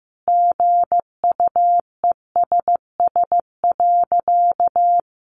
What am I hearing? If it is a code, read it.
Morse code, heard as a keyed tone, about 15 wpm, 700 Hz. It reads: GUESS.
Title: Sentences from Top 1500 Words 15wpm